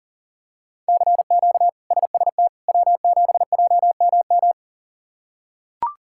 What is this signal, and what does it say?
Morse code, heard as a keyed tone, about 37 wpm, 700 Hz. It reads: CQ SST W7JMM
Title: K1USN SST QSO Practice - 40wpm